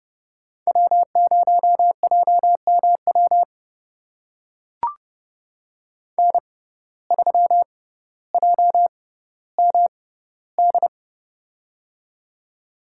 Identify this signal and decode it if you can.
Morse code, heard as a keyed tone, about 30 wpm, spaced as for 15 wpm, 700 Hz. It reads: W0JMW N3JMD
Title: USA Call Signs - 5 Characters - 6x Character Spacing - 30wpm